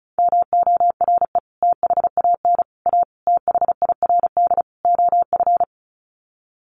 Morse code, heard as a keyed tone, about 34 wpm, 700 Hz. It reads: MORE THAN A THIRD OF
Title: Sets of 5 Words Encore2 35wpm